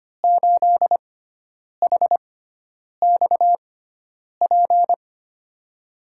Morse code, heard as a keyed tone, about 25 wpm, spaced as for 9 wpm, 700 Hz. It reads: 8HXP
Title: USA Call Signs - 6 Characters - 6x Character Spacing - 25wpm